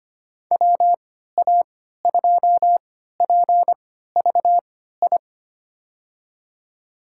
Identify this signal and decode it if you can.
Morse code, heard as a keyed tone, about 25 wpm, spaced as for 14 wpm, 700 Hz. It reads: WA2PVI
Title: USA Call Signs - 6 Characters - 3x Character Spacing - 25wpm